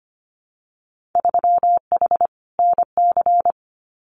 Morse code, heard as a keyed tone, about 25 wpm, 700 Hz. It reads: 3H NC
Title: Winter Field Day Exchanges 25wpm